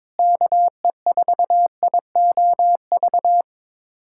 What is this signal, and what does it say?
Morse code, heard as a keyed tone, about 22 wpm, 700 Hz. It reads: KE4IOV
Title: USA Callsign DE Callsign - 6 Characters - 22wpm